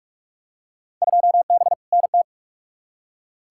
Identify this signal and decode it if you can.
Morse code, heard as a keyed tone, about 46 wpm, 700 Hz. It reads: 2B NT
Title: Field Day Exchanges Encore - 45wpm